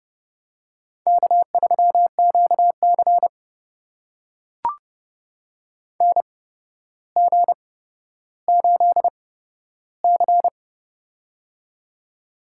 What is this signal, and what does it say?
Morse code, heard as a keyed tone, about 30 wpm, spaced as for 13 wpm, 700 Hz. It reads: K3QC NG8C
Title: USA Call Signs - 4 Characters - 8x Character Spacing - 30wpm